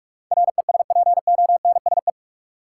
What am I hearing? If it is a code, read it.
Morse code, heard as a keyed tone, about 44 wpm, 700 Hz. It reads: RESPONSE
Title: Sentences from Top 1400 Words - New Word Drill - 45wpm